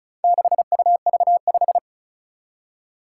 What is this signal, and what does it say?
Morse code, heard as a keyed tone, about 36 wpm, 700 Hz. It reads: 6UV5
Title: Four Challenging Letter-Number Combinations 35wpm